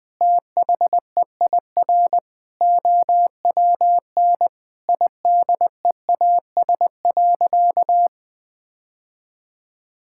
Morse code, heard as a keyed tone, about 20 wpm, 700 Hz. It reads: THEIR OWN IDEAS.
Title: Sentences with 2-Word Phrases - 20wpm